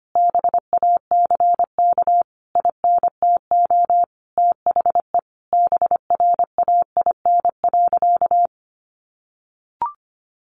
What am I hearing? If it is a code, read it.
Morse code, heard as a keyed tone, about 25 wpm, 700 Hz. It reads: BACK INTO THE BRAIN.
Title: A Sound Of Thunder 25wpm